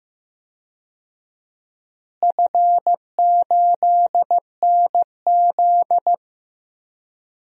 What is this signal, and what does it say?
Morse code, heard as a keyed tone, about 15 wpm, 700 Hz. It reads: F8NZ
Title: DXCC-100 Call Signs - 4 Characters - Repeated 2x - 15wpm